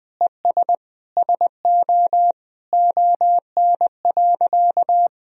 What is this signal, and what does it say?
Morse code, heard as a keyed tone, about 20 wpm, 700 Hz. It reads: ES SO ON.
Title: Sentences from Top 100 Words and Abbreviations 20wpm